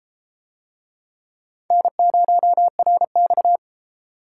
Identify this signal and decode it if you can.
Morse code, heard as a keyed tone, about 29 wpm, 700 Hz. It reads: N0RX